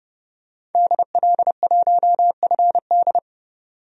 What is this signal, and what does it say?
Morse code, heard as a keyed tone, about 30 wpm, 700 Hz. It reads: DL1FD
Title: DXCC-100 Call Signs - 5 Characters - 30wpm